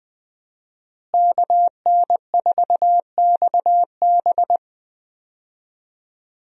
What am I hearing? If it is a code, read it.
Morse code, heard as a keyed tone, about 20 wpm, 700 Hz. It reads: KN4XB
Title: USA Call Signs - 5 Characters - Repeated 3x - 20wpm